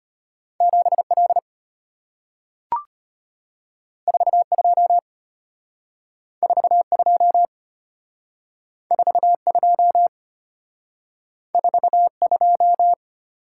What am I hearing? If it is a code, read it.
Morse code, heard as a keyed tone, about 34 wpm, 700 Hz. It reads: 7L 42 42 42 42